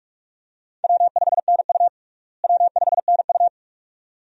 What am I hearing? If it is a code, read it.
Morse code, heard as a keyed tone, about 46 wpm, 700 Hz. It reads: W5NU W5NU
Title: USA Call Signs - 4 Characters - Repeated 2x - 45wpm